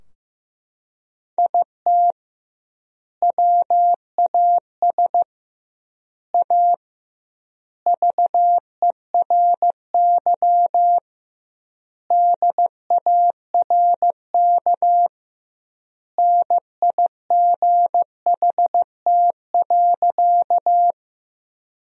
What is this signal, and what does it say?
Morse code, heard as a keyed tone, about 15 wpm, 700 Hz. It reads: IT WAS A VERY DARK NIGHT.
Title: 2x Word-Spacing - Sentences from Top 500 Words - 15wpm